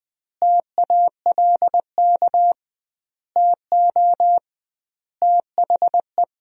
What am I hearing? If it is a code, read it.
Morse code, heard as a keyed tone, about 20 wpm, 700 Hz. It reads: TALK TO THE
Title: 2x Word-Spacing - Sentences from Top 400 Words - 20wpm